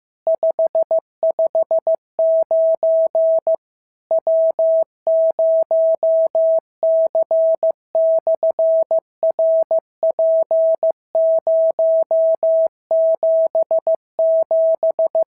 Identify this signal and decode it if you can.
Morse code, heard as a keyed tone, about 15 wpm, 650 Hz. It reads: 559 W0C/RP077
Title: SOTA QSO Practice 15wpm